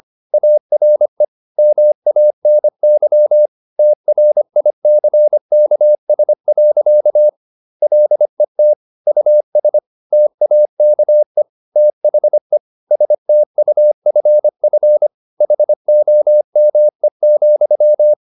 Morse code, heard as a keyed tone, about 25 wpm, 600 Hz. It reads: ARE MANY TRICKS. LET US TAKE THE STUFF HOME,
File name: the old man and the sea part 1 25wpm